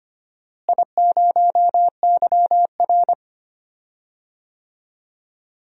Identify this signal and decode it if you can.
Morse code, heard as a keyed tone, about 25 wpm, 700 Hz. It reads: I0YR
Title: DXCC-100 Call Signs - 4 Characters - Repeated 3x - 25wpm